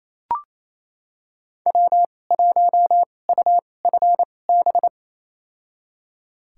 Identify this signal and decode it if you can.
Morse code, heard as a keyed tone, about 28 wpm, spaced as for 20 wpm, 700 Hz. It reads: W1UFB
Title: USA Call Signs - 5 Characters - 2x Character Spacing - 28wpm